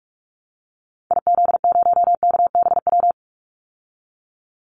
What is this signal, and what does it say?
Morse code, heard as a keyed tone, about 44 wpm, 700 Hz. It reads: IZ0KBW
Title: Call Signs Repeated 2x - 45wpm